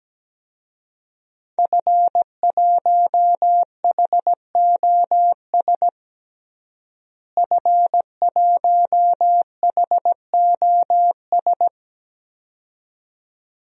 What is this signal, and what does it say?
Morse code, heard as a keyed tone, about 17 wpm, 700 Hz. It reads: F1HOS F1HOS
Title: DXCC-100 Call Signs - 5 Characters - Repeated 2x - 17wpm